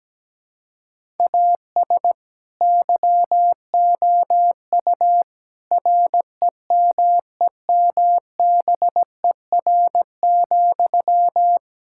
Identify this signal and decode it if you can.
Morse code, heard as a keyed tone, about 17 wpm, 700 Hz. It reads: AS YOU REMEMBER,
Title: Sentences from Top 1000 Words - Review Part 5 - 17wpm